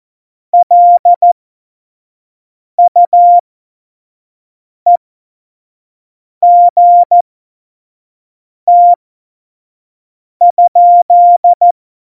Morse code, heard as a keyed tone, about 14 wpm, spaced as for 5 wpm, 700 Hz. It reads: LUEGT?
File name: Morsekurs / Witze / Witz 08. Januar 2019, 25 F70